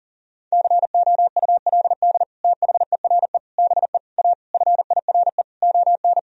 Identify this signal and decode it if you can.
Morse code, heard as a keyed tone, about 37 wpm, 700 Hz. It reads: COULD THERE BE A FIRE ON
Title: Sentences from Top 1000 Words - Review Part 4 - 40wpm